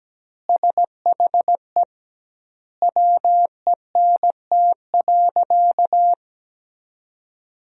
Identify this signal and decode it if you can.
Morse code, heard as a keyed tone, about 17 wpm, 700 Hz. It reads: SHE WENT.
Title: 2x Word-Spacing - Sentences from Top 200 Words - 17wpm